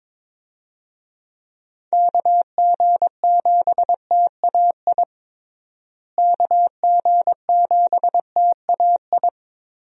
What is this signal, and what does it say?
Morse code, heard as a keyed tone, about 22 wpm, 700 Hz. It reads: KG7TAI KG7TAI
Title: USA Call Signs - 6 Characters - Repeated 3x - 22wpm